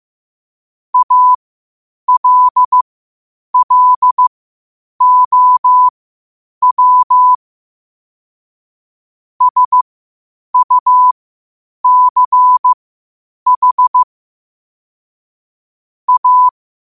Morse code, heard as a keyed tone, about 15 wpm, spaced as for 8 wpm, 1000 Hz. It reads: ALLOW SUCH A